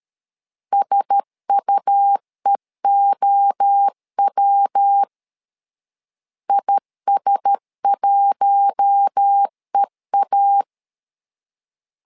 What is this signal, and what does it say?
Morse code, heard as a keyed tone, about 13 wpm, 800 Hz. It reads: SUEOW IS1EA
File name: LEZIONI / 4 Q1S 12wpm 1802